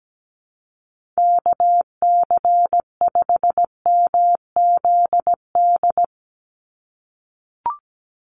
Morse code, heard as a keyed tone, about 17 wpm, 700 Hz. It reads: KC5MZD
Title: USA Call Signs 17wpm